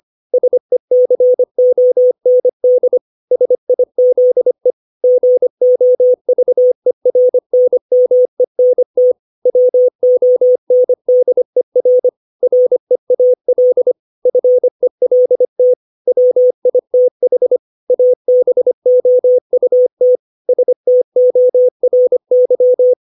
Morse code, heard as a keyed tone, about 25 wpm, 500 Hz. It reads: SECOND SIZE GOVERNMENT WONDER REAL FELT WITH ABOUT STORY
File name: 632 words / 632 Words 25WPM